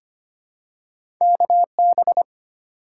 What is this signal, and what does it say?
Morse code, heard as a keyed tone, about 25 wpm, 700 Hz. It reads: KB
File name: Two Letter-Number Combinations - 25wpm